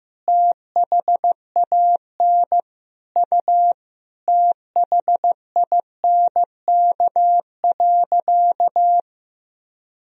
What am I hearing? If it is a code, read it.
Morse code, heard as a keyed tone, about 15 wpm, 700 Hz. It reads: THAN U THINK.
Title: Sentences from Top 400 Words and Abbreviations 15wpm